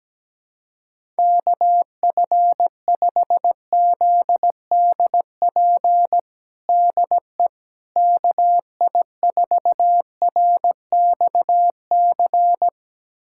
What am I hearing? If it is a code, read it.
Morse code, heard as a keyed tone, about 17 wpm, 700 Hz. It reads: KF5ZDP DE KI4RXC
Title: USA Callsign DE Callsign 17wpm